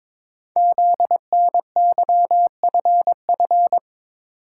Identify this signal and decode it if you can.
Morse code, heard as a keyed tone, about 22 wpm, 700 Hz. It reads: ZNYFF
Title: Code Groups - 5 Letters - 22wpm